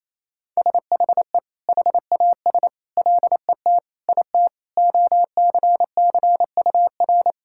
Morse code, heard as a keyed tone, about 28 wpm, 700 Hz. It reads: SHE HAS LET IT OCCUR
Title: Sentences from Top 1000 Words - Review Part 6 - 28wpm